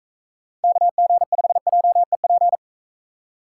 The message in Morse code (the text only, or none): KG5JEP